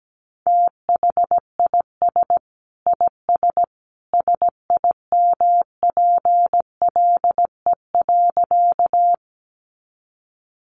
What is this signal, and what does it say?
Morse code, heard as a keyed tone, about 17 wpm, 700 Hz. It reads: THIS IS SIMPLE.
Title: Sentences from Top 400 Words 17wpm